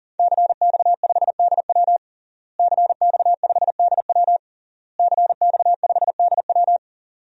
Morse code, heard as a keyed tone, about 40 wpm, 700 Hz. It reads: CX5DW CX5DW CX5DW
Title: DXCC-100 Call Signs - 5 Characters - Repeated 3x - 40wpm